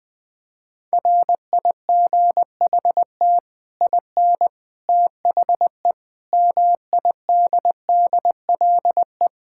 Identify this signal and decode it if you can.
Morse code, heard as a keyed tone, about 20 wpm, 700 Hz. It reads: RIGHT IN THE MIDDLE
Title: Build A Phrase - 5-Word Phrases - 20wpm